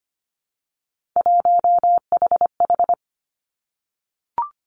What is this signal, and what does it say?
Morse code, heard as a keyed tone, about 25 wpm, 700 Hz. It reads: 1HH
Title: Three Challenging Letter-Number Combinations 25wpm